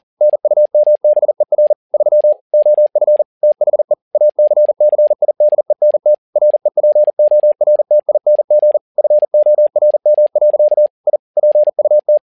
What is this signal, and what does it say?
Morse code, heard as a keyed tone, about 40 wpm, 600 Hz. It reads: NUMBER 3 OF THE ACCIDENT REPORTING FORM. I PUT
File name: learning center / tower trouble / tower touble 40wpm